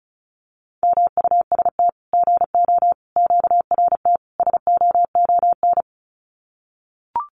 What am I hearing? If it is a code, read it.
Morse code, heard as a keyed tone, about 34 wpm, 700 Hz. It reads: MUST GO QRT SOON
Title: Common QSO Phrases 35wpm